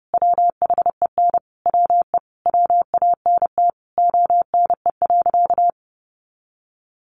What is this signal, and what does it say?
Morse code, heard as a keyed tone, about 30 wpm, 700 Hz. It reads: WHEN WE WANT ONE.
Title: Sentences from Top 1200 Words 30wpm